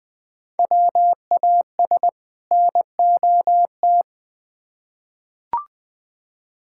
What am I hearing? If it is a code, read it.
Morse code, heard as a keyed tone, about 21 wpm, 700 Hz. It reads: WAS NOT
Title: Build A Phrase - 2-Word Phrases - 20wpm